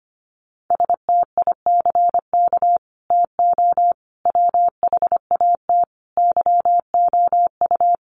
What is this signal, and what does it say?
Morse code, heard as a keyed tone, about 25 wpm, 700 Hz. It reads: STICK TO WHAT YOU
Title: Sentences from Top 1200 Words 25wpm